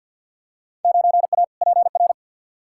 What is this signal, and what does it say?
Morse code, heard as a keyed tone, about 50 wpm, 700 Hz. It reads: 9A PR
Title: Field Day Exchanges Encore - 50wpm